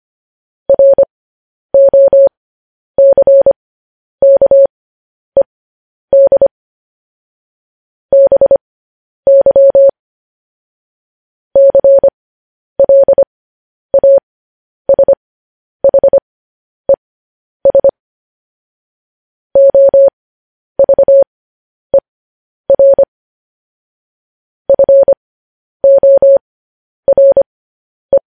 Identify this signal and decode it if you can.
Morse code, heard as a keyed tone, about 25 wpm, spaced as for 10 wpm, 550 Hz. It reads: ROCKED BY CLASHES OVER FORE
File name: podcast / News headlines-1x-25x10